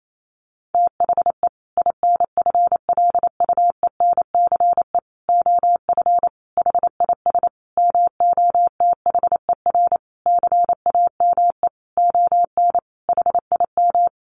Morse code, heard as a keyed tone, about 28 wpm, 700 Hz. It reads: THE INFLUENCE OF HIS MOTHER CAME ON HIM
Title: Sentences from Top 1100 Words 28wpm